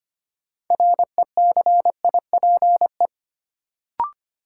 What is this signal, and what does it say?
Morse code, heard as a keyed tone, about 25 wpm, 700 Hz. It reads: RECIPE
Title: Sentences from Top 1200 Words - New Word Drill - 25wpm